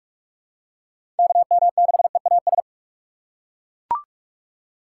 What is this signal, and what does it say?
Morse code, heard as a keyed tone, about 41 wpm, 700 Hz. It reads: KM6EAS
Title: USA Call Signs - 6 Characters - 4x Character Spacing - 45wpm